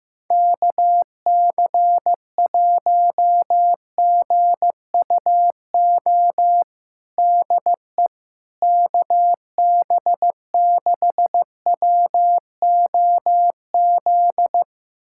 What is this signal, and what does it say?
Morse code, heard as a keyed tone, about 15 wpm, 700 Hz. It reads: KC1GUO DE KB6WOZ
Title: USA Callsign DE Callsign 15wpm